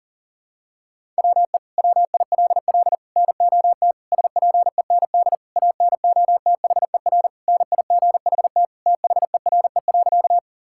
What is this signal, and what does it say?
Morse code, heard as a keyed tone, about 40 wpm, 700 Hz. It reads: WE WILL NOT SPEND ANOTHER NIGHT THERE.
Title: Sentences from Top 1000 Words - Review Part 5 - 40wpm